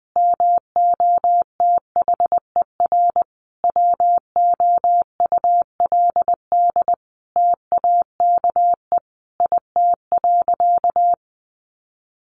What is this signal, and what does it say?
Morse code, heard as a keyed tone, about 19 wpm, 700 Hz. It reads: MOTHER WOULD TAKE IT.
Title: Sentences from Top 700 Words 20wpm